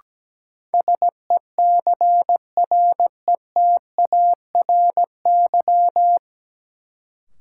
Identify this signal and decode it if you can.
Morse code, heard as a keyed tone, about 17 wpm, 700 Hz. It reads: SECRETARY
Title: Sentences from Top 1300 Words - New Word Drill - 17wpm